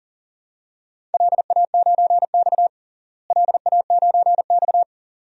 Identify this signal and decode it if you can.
Morse code, heard as a keyed tone, about 37 wpm, 700 Hz. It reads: LA9X LA9X
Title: DXCC-100 Call Signs - 4 Characters - Repeated 2x - 40wpm